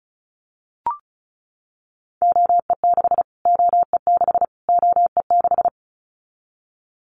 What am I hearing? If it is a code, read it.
Morse code, heard as a keyed tone, about 34 wpm, 700 Hz. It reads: OE6 OE6 OE6
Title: Callsign Prefixes for Top 25 DXCC - Repeated 3x - 35wpm